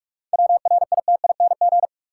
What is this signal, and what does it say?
Morse code, heard as a keyed tone, about 45 wpm, 700 Hz. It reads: WRITING
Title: Sentences from Top 800 Words - New Word Drill - 45wpm